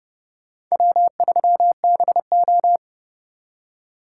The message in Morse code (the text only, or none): W3BO